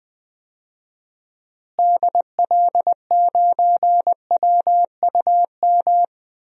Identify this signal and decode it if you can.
Morse code, heard as a keyed tone, about 20 wpm, 700 Hz. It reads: DL9WUM
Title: DXCC-100 Call Signs - 6 Characters - Repeated 2x - 20wpm